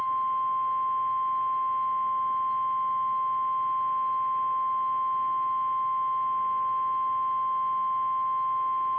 Emergency Alert System (SAME) tones (the 1050 Hz attention tone). It is heard from 0.5 to 8.5 s.
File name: eas / KKPRKXPN / KKPRKXPN